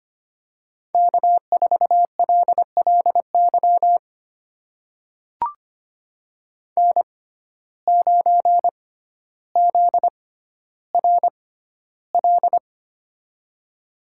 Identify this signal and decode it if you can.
Morse code, heard as a keyed tone, about 25 wpm, spaced as for 13 wpm, 700 Hz. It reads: K4LLY N9ZRL
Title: USA Call Signs - 6x Character Spacing - 25wpm